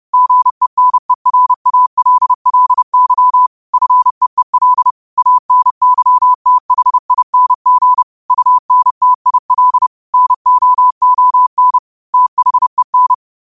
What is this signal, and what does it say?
Morse code, heard as a keyed tone, about 30 wpm, 1000 Hz. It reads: GENERALLY FEEL ANYTHING UNTIL NOON THEN
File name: qotdfiles / 30wpm